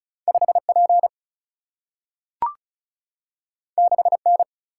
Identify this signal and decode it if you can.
Morse code, heard as a keyed tone, about 36 wpm, 700 Hz. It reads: 5P 6N
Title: Single-Letter Single-Number 35wpm